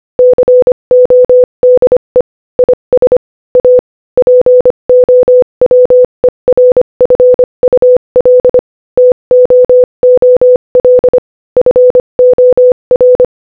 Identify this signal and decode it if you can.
Morse code, heard as a keyed tone, about 25 wpm, 500 Hz. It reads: CODE IS A POWERFUL TOOL FOR
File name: Morse-Code-Audio-Speed-25-10